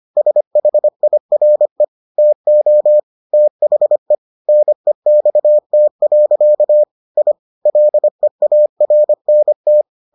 Morse code, heard as a keyed tone, about 25 wpm, 600 Hz. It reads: SHIRE TO THE NEXT. I LEARNT